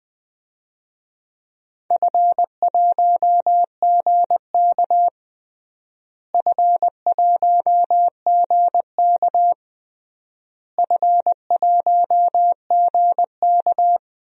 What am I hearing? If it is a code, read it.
Morse code, heard as a keyed tone, about 20 wpm, 700 Hz. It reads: F1GK F1GK F1GK
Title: DXCC-100 Call Signs - 4 Characters - Repeated 3x - 20wpm